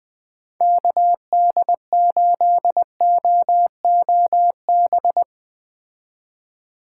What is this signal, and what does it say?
Morse code, heard as a keyed tone, about 20 wpm, 700 Hz. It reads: KD8OOB